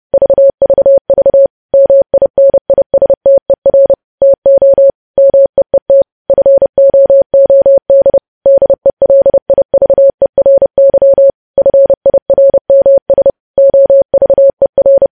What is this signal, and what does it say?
Morse code, heard as a keyed tone, about 30 wpm, 550 Hz. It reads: VVV MINISTER TO MEET FOOD DELIVERY FIRMS OVER
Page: News Headlines in Morse Code at 30 WPM